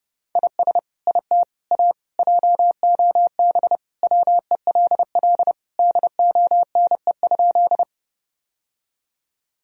Morse code, heard as a keyed tone, about 32 wpm, 700 Hz. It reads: IS IT A JOB WELL DONE?
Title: Sentences from Top 1000 Words - Review Part 3 - 30wpm